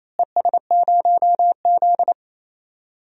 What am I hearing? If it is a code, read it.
Morse code, heard as a keyed tone, about 28 wpm, 700 Hz. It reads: ES0Z
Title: DXCC-100 Call Signs - 4 Characters - Repeated 2x - 28wpm